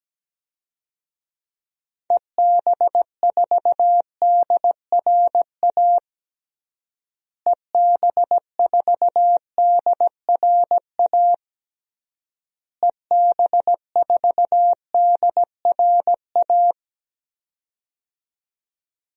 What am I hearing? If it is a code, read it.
Morse code, heard as a keyed tone, about 17 wpm, 700 Hz. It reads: EB4DRA EB4DRA EB4DRA
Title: DXCC-100 Call Signs - 6 Characters - Repeated 3x - 17wpm